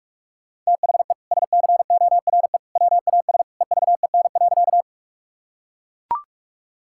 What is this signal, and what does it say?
Morse code, heard as a keyed tone, about 42 wpm, 700 Hz. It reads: THE SCORE WAS EVEN.
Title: Sentences from Top 1600 Words 45wpm